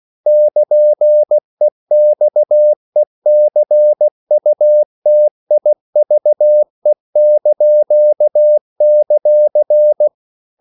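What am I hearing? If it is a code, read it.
Morse code, heard as a keyed tone, about 16 wpm, 600 Hz. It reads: <KN>EXECUTIVE);
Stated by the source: CW practice of national capitals at 16 words per minute, from ON6ZQ.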